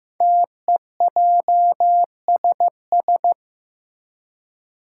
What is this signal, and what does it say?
Morse code, heard as a keyed tone, about 15 wpm, 700 Hz. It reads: TEJSS
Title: Code Groups - 5 Letters - 15wpm